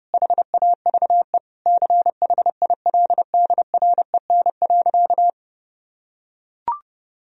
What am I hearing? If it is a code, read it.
Morse code, heard as a keyed tone, about 30 wpm, 700 Hz. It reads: HAVE CHILDREN.
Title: Sentences from Top 1000 Words - Review Part 1 - 30wpm